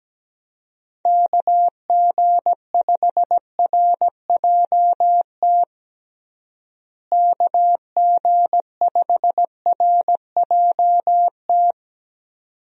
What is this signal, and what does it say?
Morse code, heard as a keyed tone, about 17 wpm, 700 Hz. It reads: KG5RJT KG5RJT
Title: USA Call Signs - 6 Characters - Repeated 2x - 17wpm